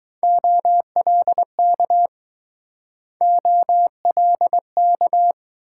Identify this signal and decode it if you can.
Morse code, heard as a keyed tone, about 20 wpm, 700 Hz. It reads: OLK OLK